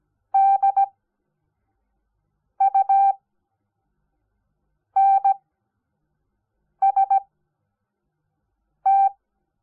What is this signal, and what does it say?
Morse code, heard as a keyed tone, about 17 wpm, spaced as for 5 wpm, 780 Hz. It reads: DUNST
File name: Onlinelehrgang / morsen / DJ4UF Morsen05